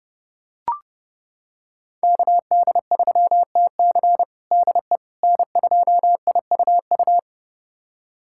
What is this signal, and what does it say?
Morse code, heard as a keyed tone, about 32 wpm, 700 Hz. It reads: KD3TC DE N2IUU
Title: USA Callsign DE Callsign - 5 Characters - 30wpm